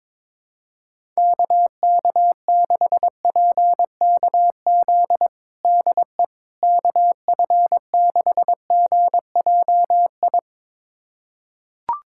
Morse code, heard as a keyed tone, about 22 wpm, 700 Hz. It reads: KK6PKZ DE KF6GJI
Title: USA Callsign DE Callsign 22wpm